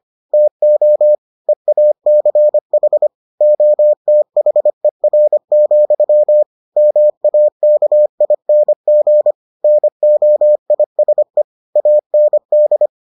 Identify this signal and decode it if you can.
Morse code, heard as a keyed tone, about 25 wpm, 600 Hz. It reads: TO EACH OTHER, MAKING NOISE AND